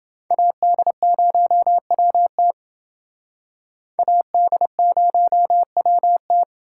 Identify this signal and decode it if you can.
Morse code, heard as a keyed tone, about 27 wpm, 700 Hz. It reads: AD0WT AD0WT